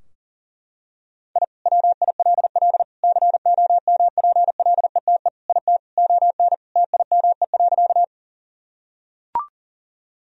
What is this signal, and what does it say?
Morse code, heard as a keyed tone, about 40 wpm, 700 Hz. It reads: I WILL COMPLETE IT ON TIME.
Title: Sentences from Top 1000 Words - Review Part 2 - 40wpm